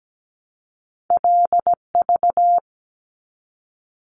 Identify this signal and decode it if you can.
Morse code, heard as a keyed tone, about 17 wpm, 700 Hz. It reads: LV
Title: Top Two Letter Combinations in Words 17wpm